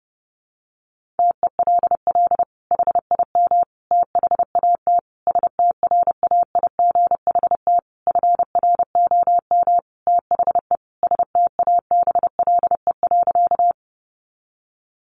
Morse code, heard as a keyed tone, about 28 wpm, 700 Hz. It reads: TELL HIM THAT STRAIGHT FROM THE STABLE.
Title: Sentences from Top 1500 Words 30wpm